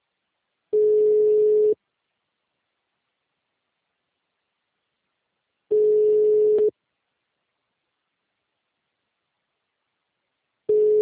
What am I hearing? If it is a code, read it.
Ringback tone (Europe, 425 Hz).